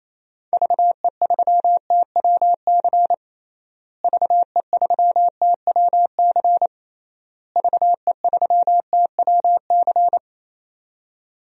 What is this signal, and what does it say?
Morse code, heard as a keyed tone, about 28 wpm, 700 Hz. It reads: VE3TWC VE3TWC VE3TWC
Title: DXCC-100 Call Signs - 6 Characters - Repeated 3x - 28wpm